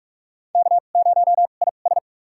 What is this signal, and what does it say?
Morse code, heard as a keyed tone, about 48 wpm, spaced as for 33 wpm, 700 Hz. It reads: K0IS
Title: USA Call Signs - 4 Characters - 2x Character Spacing - 45wpm